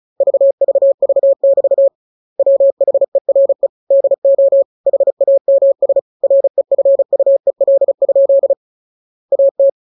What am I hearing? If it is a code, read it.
Morse code, heard as a keyed tone, about 36 wpm, 550 Hz. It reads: VVV<BT> WHERE DO HAMS REFUEL? AT